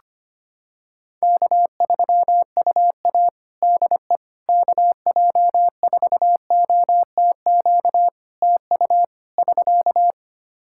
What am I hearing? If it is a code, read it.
Morse code, heard as a keyed tone, about 25 wpm, 700 Hz. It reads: K3UA DE KJ4OTQ TU <SK>
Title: Practice QSO - Exchange 4 - Ending - 25wpm